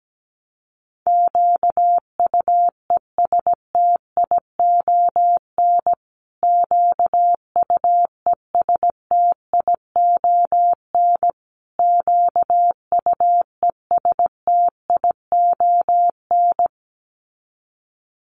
Morse code, heard as a keyed tone, about 17 wpm, 700 Hz. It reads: QUESTION QUESTION QUESTION
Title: Top 500 Words Repeated 17wpm